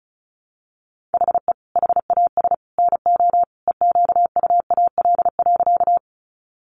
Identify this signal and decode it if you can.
Morse code, heard as a keyed tone, about 34 wpm, 700 Hz. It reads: HE HAS NO EQUAL.
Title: Sentences from Top 1400 Words 35wpm